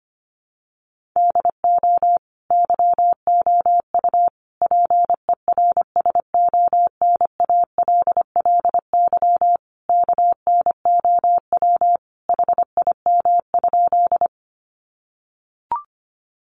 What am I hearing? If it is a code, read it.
Morse code, heard as a keyed tone, about 25 wpm, 700 Hz. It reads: DO YOU PERSONALLY KNOW HIM?
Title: Sentences from Top 1400 Words 25wpm